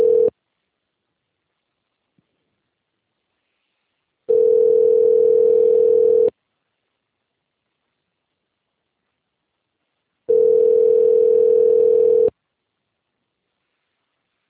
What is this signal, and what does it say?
Ringback tone (North America), 440 + 480 Hz.
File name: rg-1600-unknown- - - .27117